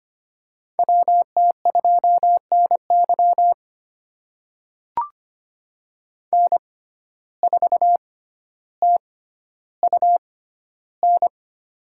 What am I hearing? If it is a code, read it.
Morse code, heard as a keyed tone, about 25 wpm, spaced as for 13 wpm, 700 Hz. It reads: WT2NY N4TUN
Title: USA Call Signs - 5 Characters - 6x Character Spacing - 25wpm